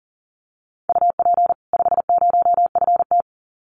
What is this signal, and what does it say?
Morse code, heard as a keyed tone, about 40 wpm, 700 Hz. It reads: UP 50FT
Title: Common QSO Phrases 40wpm